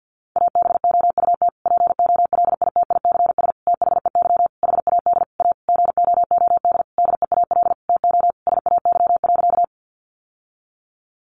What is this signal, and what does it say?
Morse code, heard as a keyed tone, about 48 wpm, 700 Hz. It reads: ABOUT POLITICS THEY HAD A GOOD DEAL TO SAY.
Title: Sentences from Top 1000 Words 50wpm